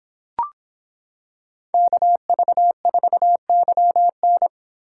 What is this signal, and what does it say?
Morse code, heard as a keyed tone, about 26 wpm, 700 Hz. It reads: KV4YN